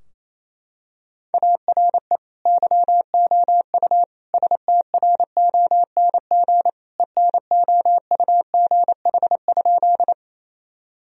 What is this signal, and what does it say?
Morse code, heard as a keyed tone, about 28 wpm, 700 Hz. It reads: ARE YOU STRONG ENOUGH?
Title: Sentences from Top 1000 Words - Review Part 3 - 28wpm